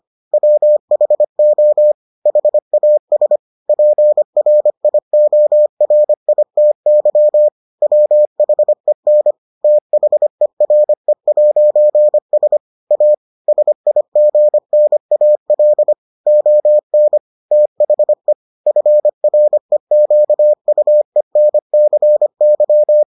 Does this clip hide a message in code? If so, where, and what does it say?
Yes: Morse code from 0.3 to 23.0 s, heard as a keyed tone, about 25 wpm, 600 Hz. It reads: WHO HAS PRIORITY WHEN THERE'S A SIGNAL ON THE FREQUENCY